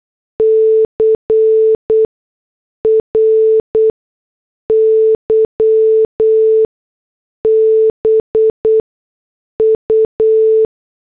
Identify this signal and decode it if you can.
Morse code, heard as a keyed tone, about 8 wpm, spaced as for 6 wpm, 440 Hz. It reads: CRY BU